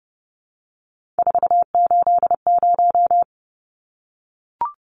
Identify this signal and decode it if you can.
Morse code, heard as a keyed tone, about 30 wpm, 700 Hz. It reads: 480
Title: Numbers - 3-Digit 30wpm